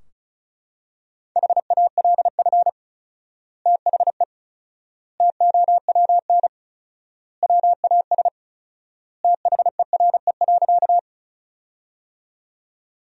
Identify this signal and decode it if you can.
Morse code, heard as a keyed tone, about 34 wpm, 700 Hz. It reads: HALF THE TOWN WAS THERE.
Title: 4x Word-Spacing - Sentences from Top 500 Words - 35wpm